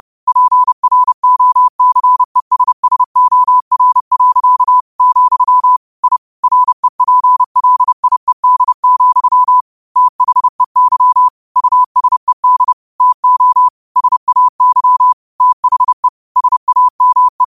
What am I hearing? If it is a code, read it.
Morse code, heard as a keyed tone, about 30 wpm, 1000 Hz. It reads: PROCESSOR. , I REPLIED, THEY USED TO SAY THE SAME
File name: qotdfiles / 30wpm